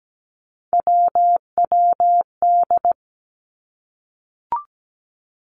Morse code, heard as a keyed tone, about 17 wpm, 700 Hz. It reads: WWD
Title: Three Challenging Letter-Number Combinations 17wpm